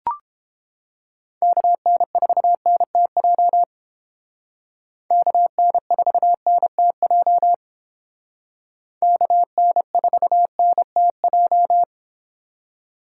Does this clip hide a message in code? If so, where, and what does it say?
Yes: Morse code from 1.4 to 11.8 s, heard as a keyed tone, about 30 wpm, 700 Hz. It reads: KN4NTJ KN4NTJ KN4NTJ